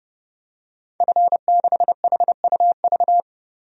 Morse code, heard as a keyed tone, about 30 wpm, 700 Hz. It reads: F6HUV
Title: DXCC-100 Call Signs - 5 Characters - 30wpm